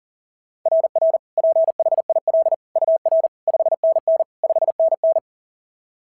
Morse code, heard as a keyed tone, about 40 wpm, 690 Hz. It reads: RR PHIL UR 5NN 5NN
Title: SOTA QSO Practice 40wpm